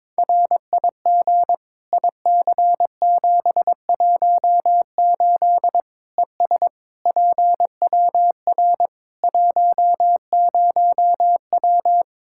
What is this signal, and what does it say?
Morse code, heard as a keyed tone, about 22 wpm, 700 Hz. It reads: RIG IC718 ES PWR 10W
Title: Practice QSO - Exchange 2 - Info - 22wpm